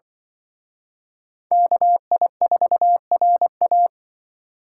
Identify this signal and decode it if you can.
Morse code, heard as a keyed tone, about 23 wpm, 700 Hz. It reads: KI4RA
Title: Speed Racing - USA 2x2 Call Signs 20wpm